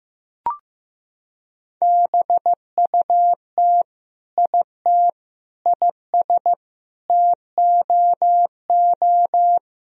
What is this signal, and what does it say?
Morse code, heard as a keyed tone, about 16 wpm, 700 Hz. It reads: BUT IT IS TOO
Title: Sentences from Top 1000 Words - Review Part 1 - 15wpm